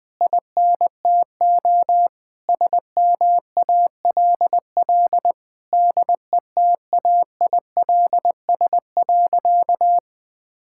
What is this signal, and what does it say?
Morse code, heard as a keyed tone, about 20 wpm, 700 Hz. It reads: INTO SMALL DETAILS.
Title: Sentences with 9-Word Phrases - 20wpm